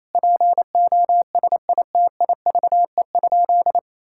Morse code, heard as a keyed tone, about 28 wpm, 700 Hz. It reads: POSITIVE?
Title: Sentences from Top 1000 Words - Review Part 5 - 28wpm